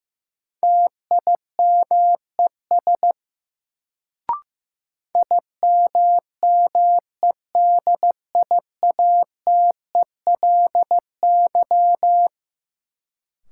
Morse code, heard as a keyed tone, about 15 wpm, 700 Hz. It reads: TIMES IMMEDIATELY
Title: Sentences from Top 1100 Words - New Word Drill Alt - 15wpm